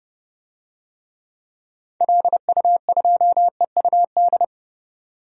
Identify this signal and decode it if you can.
Morse code, heard as a keyed tone, about 32 wpm, 700 Hz. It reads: LU2EUD
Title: DXCC-100 Call Signs - 6 Characters - 30wpm